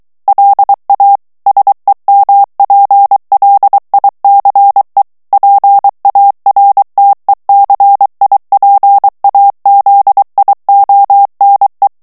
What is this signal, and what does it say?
Morse code, heard as a keyed tone, about 23 wpm, 800 Hz. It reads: LA SEMPLICE PARTECIPAZIONE
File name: ITALIEN texte clair 2 vitesse 22 mots minute